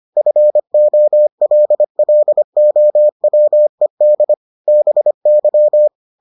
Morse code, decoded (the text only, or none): FOLLOWED BY